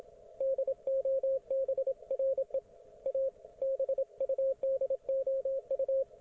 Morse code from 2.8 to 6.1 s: A BUDOU